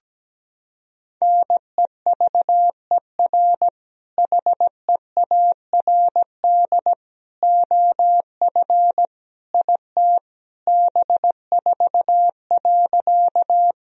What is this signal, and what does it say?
Morse code, heard as a keyed tone, about 17 wpm, 700 Hz. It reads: NEVER HEARD OF IT B4.
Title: Sentences from Top 400 Words and Abbreviations 17wpm